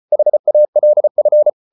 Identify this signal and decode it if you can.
Morse code, heard as a keyed tone, about 34 wpm, 600 Hz. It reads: HALF